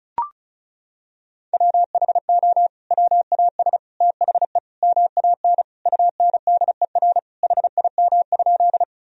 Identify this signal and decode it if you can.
Morse code, heard as a keyed tone, about 36 wpm, 700 Hz. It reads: WHO WAS THE MAN UNDER HIM?
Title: Sentences from Top 1000 Words - Review Part 1 - 35wpm